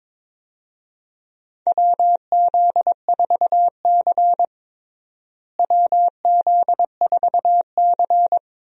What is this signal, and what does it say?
Morse code, heard as a keyed tone, about 22 wpm, 700 Hz. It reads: WZ4C WZ4C
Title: USA Call Signs - 4 Characters - Repeated 2x - 22wpm